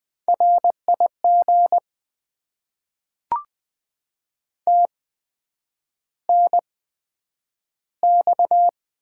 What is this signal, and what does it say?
Morse code, heard as a keyed tone, about 20 wpm, spaced as for 9 wpm, 700 Hz. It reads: RIG TNX
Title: Top 20 QSO Elements - 8x Character Spacing - 20wpm